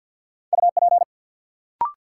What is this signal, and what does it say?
Morse code, heard as a keyed tone, about 53 wpm, 700 Hz. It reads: UP
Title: Sentences from Top 100 Words - New Word Drill - 50wpm